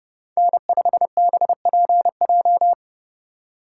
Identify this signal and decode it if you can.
Morse code, heard as a keyed tone, about 31 wpm, 700 Hz. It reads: N5BPJ
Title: USA Call Signs - 2x Character Spacing - 30wpm